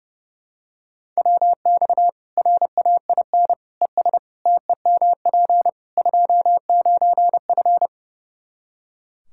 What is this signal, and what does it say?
Morse code, heard as a keyed tone, about 30 wpm, 700 Hz. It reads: WX RAIN ES TEMP 29F
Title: Practice QSO - Exchange 2 - Info - 30wpm